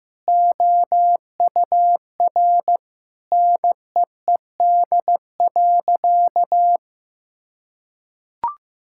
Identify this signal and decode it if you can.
Morse code, heard as a keyed tone, about 16 wpm, 700 Hz. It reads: OUR NEED.
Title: Sentences from Top 1000 Words - Review Part 6 - 15wpm